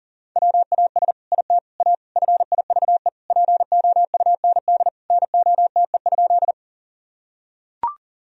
Morse code, heard as a keyed tone, about 40 wpm, 700 Hz. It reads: WAS IT A FIVE POUND NOTE?
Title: Sentences from Top 1000 Words - Review Part 2 - 40wpm